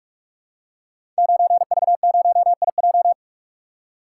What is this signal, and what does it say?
Morse code, heard as a keyed tone, about 44 wpm, 700 Hz. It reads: 9V0IJ
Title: Code Groups - 5 Letter:Numbers - 45wpm